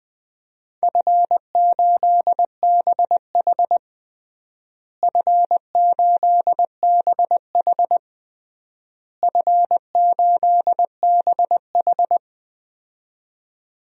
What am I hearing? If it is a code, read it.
Morse code, heard as a keyed tone, about 21 wpm, 700 Hz. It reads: F8BH F8BH F8BH
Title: DXCC-100 Call Signs - 4 Characters - Repeated 3x - 20wpm